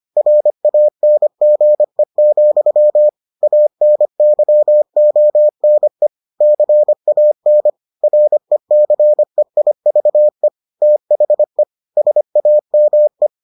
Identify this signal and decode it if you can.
Morse code, heard as a keyed tone, about 25 wpm, 600 Hz. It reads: RANGE, ANYONE CAN RECEIVE THE SAME